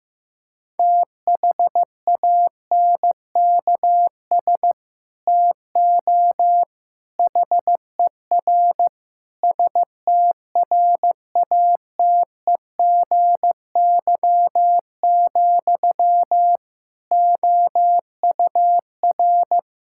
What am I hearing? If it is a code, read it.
Morse code, heard as a keyed tone, about 14 wpm, 700 Hz. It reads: THANKS TO HER STRATEGY, OUR
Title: Sentences from Top 1000 Words - Review Part 5 - 15wpm